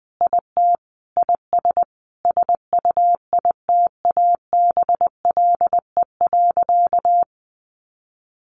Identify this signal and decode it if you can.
Morse code, heard as a keyed tone, about 20 wpm, 700 Hz. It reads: IT IS SUITABLE.
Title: Sentences from Top 1400 Words 20wpm